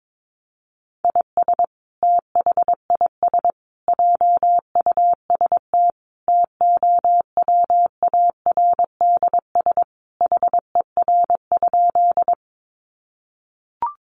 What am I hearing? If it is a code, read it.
Morse code, heard as a keyed tone, about 22 wpm, 700 Hz. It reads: IS THIS JUST TOWARDS HER?
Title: Sentences from Top 1100 Words 22wpm